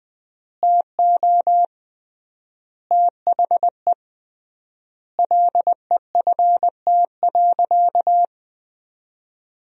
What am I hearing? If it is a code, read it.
Morse code, heard as a keyed tone, about 20 wpm, 700 Hz. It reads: TO THE LEFT.
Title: 3x Word-Spacing - Sentences from Top 300 Words - 20wpm